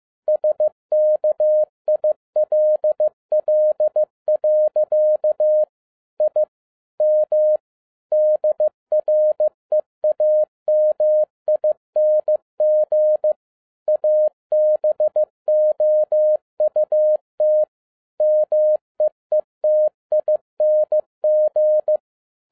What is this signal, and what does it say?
Morse code, heard as a keyed tone, about 15 wpm, 610 Hz. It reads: SKILL. I M DREAMING ABOUT MEETING
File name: 2011-07-mp3 / n02 JF2PDA 15wpm